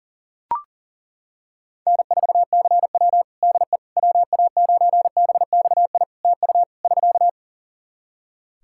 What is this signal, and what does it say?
Morse code, heard as a keyed tone, about 37 wpm, 700 Hz. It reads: N4CW DE WA9BXI TU <SK>
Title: Practice QSO - Exchange 4 - Ending - 40wpm